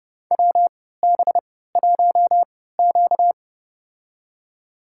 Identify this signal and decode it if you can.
Morse code, heard as a keyed tone, about 30 wpm, spaced as for 17 wpm, 700 Hz. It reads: WB1Q
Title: USA Call Signs - 4 Characters - 3x Character Spacing - 30wpm